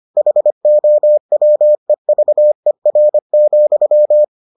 Morse code, heard as a keyed tone, about 25 wpm, 600 Hz. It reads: HOWEVER,